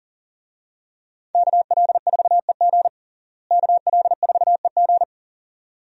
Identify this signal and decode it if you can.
Morse code, heard as a keyed tone, about 40 wpm, 700 Hz. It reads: KL4EG KL4EG
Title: USA Call Signs - 5 Characters - Repeated 2x - 40wpm